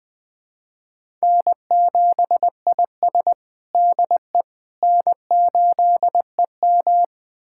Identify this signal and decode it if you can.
Morse code, heard as a keyed tone, about 20 wpm, 700 Hz. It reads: N7IS DE N8EM
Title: USA Callsign DE Callsign - 4 Characters - 20wpm